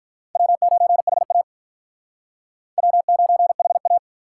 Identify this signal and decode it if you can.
Morse code, heard as a keyed tone, about 45 wpm, 700 Hz. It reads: W9HA W9HA